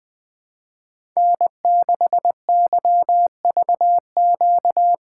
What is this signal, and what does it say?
Morse code, heard as a keyed tone, about 20 wpm, 700 Hz. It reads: N6YVQ
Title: USA Call Signs - 5 Characters - 3x Character Spacing - 20wpm